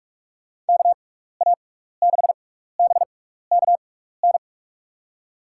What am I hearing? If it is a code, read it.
Morse code, heard as a keyed tone, about 42 wpm, spaced as for 15 wpm, 700 Hz. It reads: KA6BKN
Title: USA Call Signs - 6 Characters - 6x Character Spacing - 45wpm